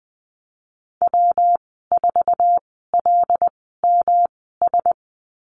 Morse code, heard as a keyed tone, about 19 wpm, spaced as for 14 wpm, 700 Hz. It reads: W4LMS
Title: 167 Lesson 27g - Number 7 - Call Signs 20wpm